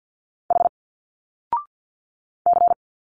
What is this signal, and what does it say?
Morse code, heard as a keyed tone, about 48 wpm, 700 Hz. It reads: H C
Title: Single Letters 50wpm